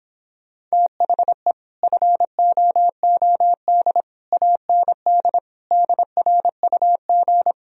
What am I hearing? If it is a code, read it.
Morse code, heard as a keyed tone, about 26 wpm, 700 Hz. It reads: THE FOOD AND DRUG